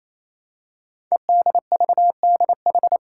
Morse code, heard as a keyed tone, about 27 wpm, 700 Hz. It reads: EDVDH
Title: Code Groups - 5 Letters - 28wpm